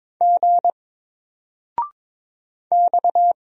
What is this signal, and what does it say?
Morse code, heard as a keyed tone, about 22 wpm, 700 Hz. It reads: G X
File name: Single-Letter-Number-22wpm